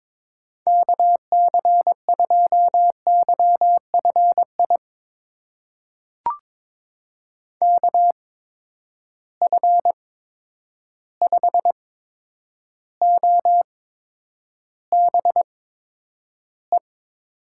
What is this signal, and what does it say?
Morse code, heard as a keyed tone, about 22 wpm, spaced as for 9 wpm, 700 Hz. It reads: KC2YFI KF5OBE
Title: USA Call Signs - 6 Characters - 8x Character Spacing - 22wpm